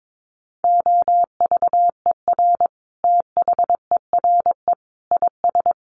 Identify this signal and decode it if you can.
Morse code, heard as a keyed tone, about 22 wpm, 700 Hz. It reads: OVER THERE IS
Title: Sentences from Top 1200 Words 22wpm